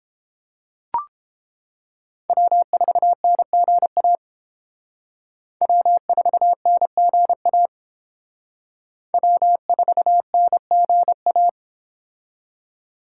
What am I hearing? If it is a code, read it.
Morse code, heard as a keyed tone, about 29 wpm, 700 Hz. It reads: W4NGA W4NGA W4NGA